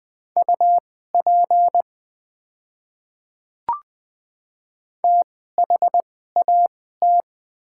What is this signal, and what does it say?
Morse code, heard as a keyed tone, about 20 wpm, spaced as for 14 wpm, 700 Hz. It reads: UP THAT
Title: 2x Character Spacing - Top 100 Words - 20wpm